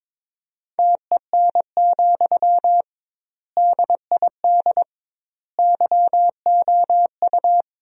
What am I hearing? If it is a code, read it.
Morse code, heard as a keyed tone, about 22 wpm, 700 Hz. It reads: TEN, DID YOU
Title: 2x Word-Spacing - Sentences from Top 400 Words - 22wpm